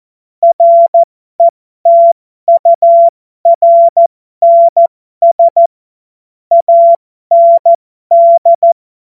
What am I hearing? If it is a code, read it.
Morse code, heard as a keyed tone, about 14 wpm, spaced as for 12 wpm, 680 Hz. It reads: RETURNS AND
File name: Morsekurs / Witze